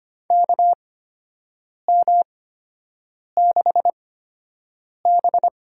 Morse code, heard as a keyed tone, about 25 wpm, spaced as for 7 wpm, 700 Hz. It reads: KM6B
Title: USA Call Signs - 6 Characters - 8x Character Spacing - 25wpm